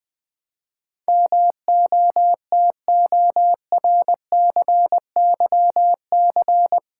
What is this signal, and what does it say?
Morse code, heard as a keyed tone, about 21 wpm, 700 Hz. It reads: MOTORCYC
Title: Build A Word - 4 Letters At A Time - 10-Letter Words - 20wpm